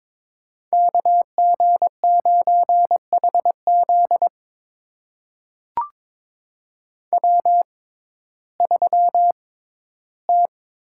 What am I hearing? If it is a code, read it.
Morse code, heard as a keyed tone, about 22 wpm, spaced as for 13 wpm, 700 Hz. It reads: KG9HZ W3T
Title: USA Call Signs - 5 Characters - 6x Character Spacing - 22wpm